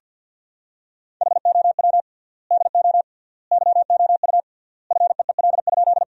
Morse code, heard as a keyed tone, about 50 wpm, 700 Hz. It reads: HOW DO YOU FEEL?
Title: 3x Word-Spacing - Sentences from Top 400 Words - 50wpm